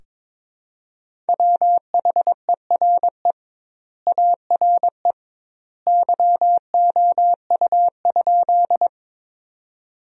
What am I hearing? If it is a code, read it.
Morse code, heard as a keyed tone, about 22 wpm, 700 Hz. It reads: WHERE ARE YOU?
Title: 2x Word-Spacing - Sentences from Top 200 Words - 22wpm